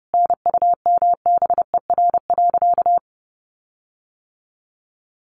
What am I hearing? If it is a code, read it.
Morse code, heard as a keyed tone, about 30 wpm, 700 Hz. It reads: NUMBER.
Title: Sentences from Top 1400 Words 30wpm